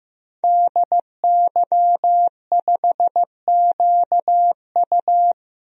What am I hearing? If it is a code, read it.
Morse code, heard as a keyed tone, about 15 wpm, 700 Hz. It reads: DY5QU
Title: DXCC-100 Call Signs - 5 Characters - 15wpm